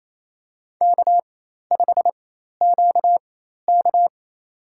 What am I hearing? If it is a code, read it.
Morse code, heard as a keyed tone, about 28 wpm, spaced as for 13 wpm, 700 Hz. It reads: K5QK
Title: USA Call Signs - 4 Characters - 4x Character Spacing - 28wpm